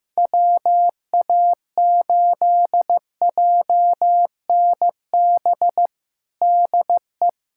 Morse code, heard as a keyed tone, about 15 wpm, 700 Hz. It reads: WA8JNB DE
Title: USA Callsign DE Callsign - 6 Characters - 15wpm